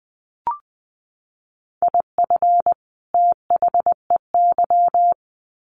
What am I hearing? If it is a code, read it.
Morse code, heard as a keyed tone, about 20 wpm, 700 Hz. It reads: IF THEY
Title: Sets of 2 Words 20wpm